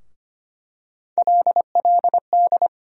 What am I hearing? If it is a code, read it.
Morse code, heard as a keyed tone, about 25 wpm, 700 Hz. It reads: LLD
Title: USA Call Sign Postfixes - 3 Letters 25wpm